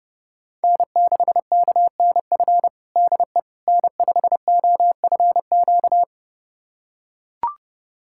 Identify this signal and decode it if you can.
Morse code, heard as a keyed tone, about 30 wpm, 700 Hz. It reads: N6KNF DE N5OFQ
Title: USA Callsign DE Callsign - 5 Characters - 30wpm